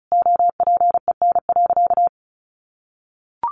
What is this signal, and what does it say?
Morse code, heard as a keyed tone, about 34 wpm, 700 Hz. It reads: OPEN.
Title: Sentences from Top 1300 Words 35wpm